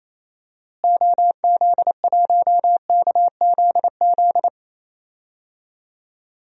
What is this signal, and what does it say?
Morse code, heard as a keyed tone, about 28 wpm, 700 Hz. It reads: OZ1KZZ
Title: DXCC-100 Call Signs - 6 Characters - Repeated 2x - 28wpm